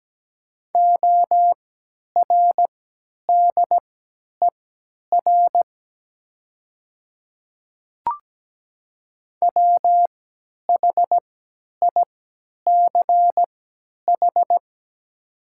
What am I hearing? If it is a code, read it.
Morse code, heard as a keyed tone, about 17 wpm, spaced as for 10 wpm, 700 Hz. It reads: ORDER WHICH
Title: 3x Character Spacing - Top 500 Words - 17wpm